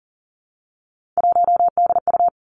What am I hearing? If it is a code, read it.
Morse code, heard as a keyed tone, about 39 wpm, 700 Hz. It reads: 1DU
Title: Three Challenging Letter-Number Combinations 40wpm